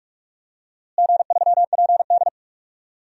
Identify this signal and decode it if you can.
Morse code, heard as a keyed tone, about 46 wpm, 700 Hz. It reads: G3PD
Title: DXCC-100 Call Signs - 4 Characters - Repeated 2x - 45wpm